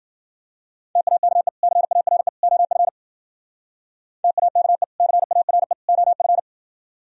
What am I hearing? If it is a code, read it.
Morse code, heard as a keyed tone, about 52 wpm, 700 Hz. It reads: TAKE CARE OF TAKE CARE OF